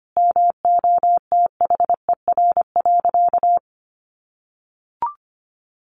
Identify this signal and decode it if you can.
Morse code, heard as a keyed tone, about 25 wpm, 700 Hz. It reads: MOTHER.
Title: Sentences from Top 800 Words 25wpm